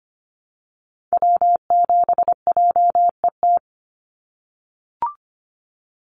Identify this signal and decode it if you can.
Morse code, heard as a keyed tone, about 25 wpm, 700 Hz. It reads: W7JET
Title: NA SOTA Activator Callsigns 25wpm